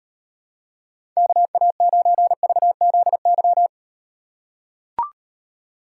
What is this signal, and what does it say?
Morse code, heard as a keyed tone, about 38 wpm, 700 Hz. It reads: KA9VZY